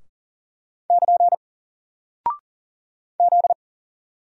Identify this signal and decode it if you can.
Morse code, heard as a keyed tone, about 40 wpm, 700 Hz. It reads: <KN> Z
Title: Single Character (Letter:Number:Punctuation:Prosign) Alt 40wpm